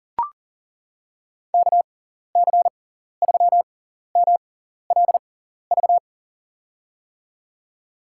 Morse code, heard as a keyed tone, about 40 wpm, spaced as for 14 wpm, 700 Hz. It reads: KC3MLV
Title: USA Call Signs - 6 Characters - 6x Character Spacing - 40wpm